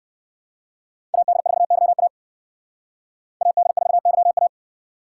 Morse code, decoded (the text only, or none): AD4YA AD4YA